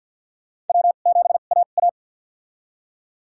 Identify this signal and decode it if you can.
Morse code, heard as a keyed tone, about 49 wpm, spaced as for 35 wpm, 700 Hz. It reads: W7AA
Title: USA Call Signs - 4 Characters - 2x Character Spacing - 50wpm